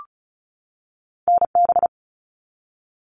Morse code, heard as a keyed tone, about 36 wpm, 700 Hz. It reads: NB
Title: Canadian Provinces and Territories 35wpm